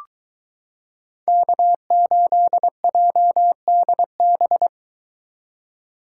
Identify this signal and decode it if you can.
Morse code, heard as a keyed tone, about 22 wpm, 700 Hz. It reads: K8JDB